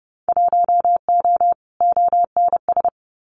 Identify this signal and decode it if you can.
Morse code, heard as a keyed tone, about 30 wpm, 700 Hz. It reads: 1O ONS
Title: Winter Field Day Exchanges 30wpm